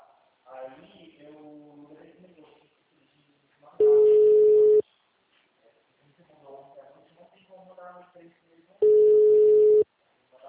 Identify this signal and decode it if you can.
Ringback tone (Europe, 425 Hz), with speech.